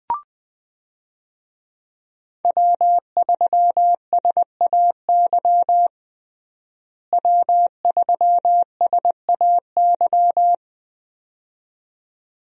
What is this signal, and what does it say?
Morse code, heard as a keyed tone, about 20 wpm, 700 Hz. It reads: W3SAY W3SAY
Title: USA Call Signs - 5 Characters - Repeated 2x - 20wpm